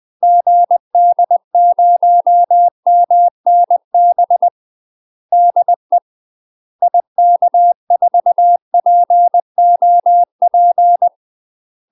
Morse code, heard as a keyed tone, about 20 wpm, 700 Hz. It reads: GD0MNB DE IK4POP
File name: UK MASTER QSOs TRAINING / 20WPM / UK MASTER QSOs 0025